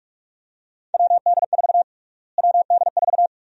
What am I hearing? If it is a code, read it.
Morse code, heard as a keyed tone, about 46 wpm, 700 Hz. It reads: WD4 WD4
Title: USA Call Sign Prefixes - Repeated 2x - 45wpm